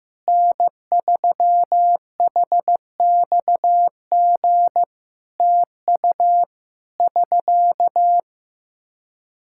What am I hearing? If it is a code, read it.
Morse code, heard as a keyed tone, about 15 wpm, 700 Hz. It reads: N3HXG TU <SK>
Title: Practice QSO - Exchange 4 - Ending - 15wpm